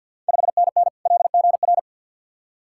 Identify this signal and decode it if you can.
Morse code, heard as a keyed tone, about 53 wpm, 690 Hz. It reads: 5NN LGR
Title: Ontario QSO Party Exchange 50wpm